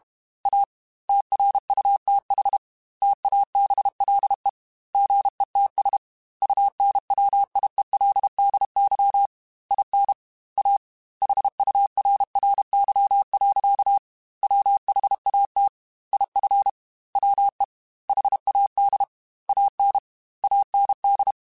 Morse code, heard as a keyed tone, about 32 wpm, 800 Hz. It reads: A TRUTH TABLE GETS UNWIELDY IN A HURRY. WHAT IF WE HAD AN AND
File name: Technical Subjects / Technical Subjects-mp3s / AND Gate @30